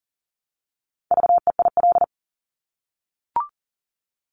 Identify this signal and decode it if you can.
Morse code, heard as a keyed tone, about 42 wpm, 700 Hz. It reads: VEIL
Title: Multi-letter Phonemes Repeated 3x 40wpm